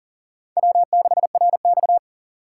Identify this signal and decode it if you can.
Morse code, heard as a keyed tone, about 37 wpm, 700 Hz. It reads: W6RX
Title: USA Call Signs - 3x Character Spacing - 40wpm